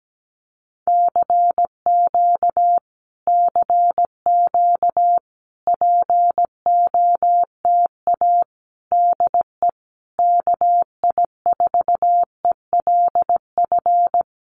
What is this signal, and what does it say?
Morse code, heard as a keyed tone, about 17 wpm, 700 Hz. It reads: CQ CQ POTA DE KI4ELF
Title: Common QSO Phrases Encore 17wpm